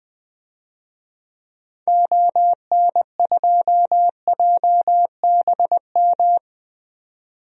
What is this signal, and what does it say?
Morse code, heard as a keyed tone, about 21 wpm, 700 Hz. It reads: ON2JBM
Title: DXCC-100 Call Signs - 6 Characters - 20wpm